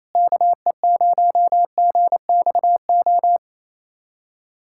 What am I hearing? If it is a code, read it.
Morse code, heard as a keyed tone, about 28 wpm, 700 Hz. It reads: KE0GXO
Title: USA Callsign DE Callsign - 6 Characters - 28wpm